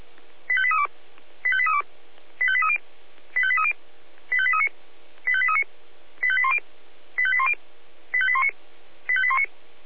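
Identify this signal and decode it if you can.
Five-tone selective call (ZVEI), heard as quick runs of five tones, tones of about 70 ms. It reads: 85832 85832 85830 85830 85839 85839 85810 85810 85819 85819
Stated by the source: ZVEI five-tone selective call received with an SDR.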